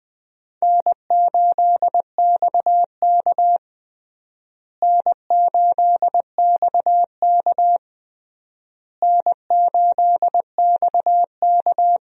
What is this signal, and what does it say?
Morse code, heard as a keyed tone, about 20 wpm, 700 Hz. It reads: N8XK N8XK N8XK
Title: USA Call Signs - 4 Characters - Repeated 3x - 20wpm